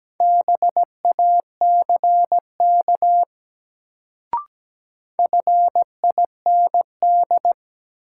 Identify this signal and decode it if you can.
Morse code, heard as a keyed tone, about 17 wpm, 700 Hz. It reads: BACK FIND
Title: Sets of 2 Words - New Word Drill - 17wpm